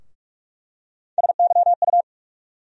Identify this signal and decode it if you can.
Morse code, heard as a keyed tone, about 46 wpm, 700 Hz. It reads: SYU
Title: USA Call Sign Postfixes - 3 Letters 45wpm